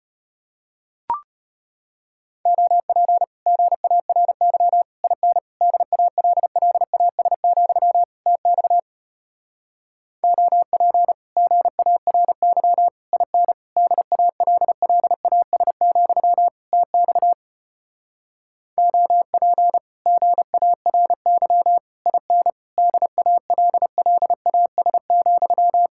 Morse code, heard as a keyed tone, about 33 wpm, 700 Hz. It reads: OP GARY IN DALLAS, TX OP GARY IN DALLAS, TX OP GARY IN DALLAS,